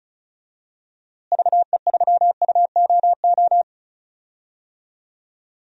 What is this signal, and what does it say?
Morse code, heard as a keyed tone, about 36 wpm, 700 Hz. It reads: VE3UOO
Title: DXCC-100 Call Signs - 6 Characters - Repeated 3x - 35wpm